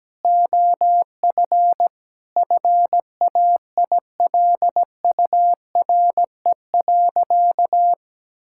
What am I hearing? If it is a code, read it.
Morse code, heard as a keyed tone, about 17 wpm, 700 Hz. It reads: OF FAILURE.
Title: Sentences from Top 1000 Words - Review Part 6 - 17wpm